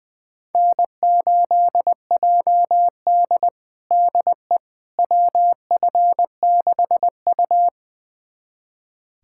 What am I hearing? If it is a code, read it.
Morse code, heard as a keyed tone, about 20 wpm, 700 Hz. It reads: N8JD DE WF6U
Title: USA Callsign DE Callsign - 4 Characters - 20wpm